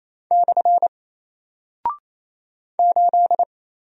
Morse code, heard as a keyed tone, about 28 wpm, 700 Hz. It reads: / 8
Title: Single Character (Letter:Number:Punctuation:Prosign) Alt 28wpm